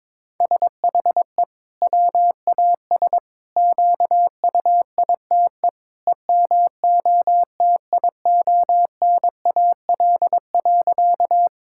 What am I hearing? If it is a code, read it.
Morse code, heard as a keyed tone, about 22 wpm, 700 Hz. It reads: SHE WAS QUITE EMOTIONAL.
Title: Sentences from Top 1000 Words - Review Part 5 - 22wpm